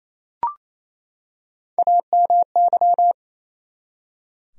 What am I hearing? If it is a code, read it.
Morse code, heard as a keyed tone, about 28 wpm, 700 Hz. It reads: AMY
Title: USA Common Names - Max 5 Letters - 28wpm